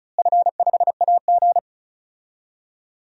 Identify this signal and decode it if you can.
Morse code, heard as a keyed tone, about 34 wpm, 700 Hz. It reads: F5AG
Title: DXCC-100 Call Signs - 4 Characters - Repeated 3x - 35wpm